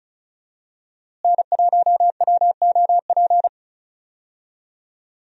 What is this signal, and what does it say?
Morse code, heard as a keyed tone, about 36 wpm, 700 Hz. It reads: N1WOP
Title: USA Call Signs - 5 Characters - Repeated 3x - 35wpm